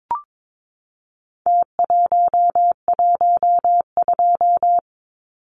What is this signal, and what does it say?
Morse code, heard as a keyed tone, about 22 wpm, 700 Hz. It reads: T112
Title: Numbers - Contest Style - 4-Digit 22wpm